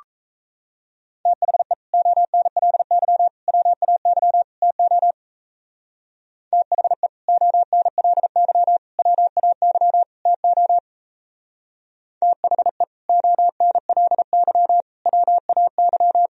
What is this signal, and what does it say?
Morse code, heard as a keyed tone, about 34 wpm, 690 Hz. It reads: THE ONLY WAY TO THE ONLY WAY TO THE ONLY WAY
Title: Speed-Racing - Sets of 4 Words 28wpm